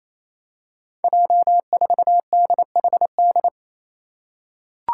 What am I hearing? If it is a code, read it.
Morse code, heard as a keyed tone, about 28 wpm, 700 Hz. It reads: J4DHD
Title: Five Challenging Letter-Number Combinations 28wpm